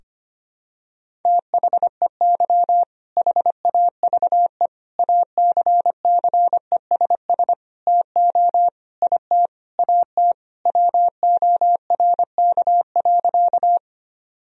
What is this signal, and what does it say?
Morse code, heard as a keyed tone, about 25 wpm, 700 Hz. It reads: THEY HAVE ACCESS TO IT AT WORK.
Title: Sentences from Top 1000 Words - Review Part 4 - 25wpm